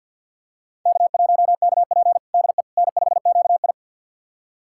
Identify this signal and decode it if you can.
Morse code, heard as a keyed tone, about 52 wpm, 700 Hz. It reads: K1KP DE N5QI
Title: USA Callsign DE Callsign - 4 Characters - 50wpm